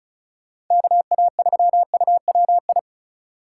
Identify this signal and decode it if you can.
Morse code, heard as a keyed tone, about 44 wpm, 700 Hz. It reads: KA3UWI